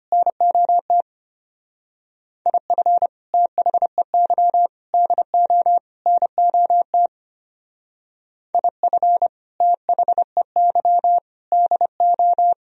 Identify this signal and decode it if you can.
Morse code, heard as a keyed tone, about 31 wpm, 700 Hz. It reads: NOT IF THEY DO NOT IF THEY DO
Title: Speed-Racing - Sets of 4 Words 25wpm